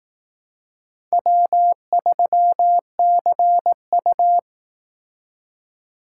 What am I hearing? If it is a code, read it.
Morse code, heard as a keyed tone, about 18 wpm, 700 Hz. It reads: W3CU
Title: Speed Racing - USA 1x2 Call Signs 15wpm